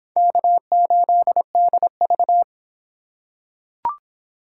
Morse code, heard as a keyed tone, about 26 wpm, 700 Hz. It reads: K8DV